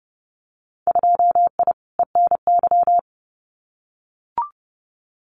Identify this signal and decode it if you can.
Morse code, heard as a keyed tone, about 30 wpm, 700 Hz. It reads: 2I ENY
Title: Winter Field Day Exchanges Encore 30wpm